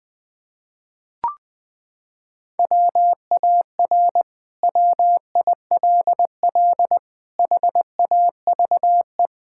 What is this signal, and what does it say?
Morse code, heard as a keyed tone, about 21 wpm, 700 Hz. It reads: WAR WILL HAVE
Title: Sentences from Top 1000 Words - Review Part 6 - 20wpm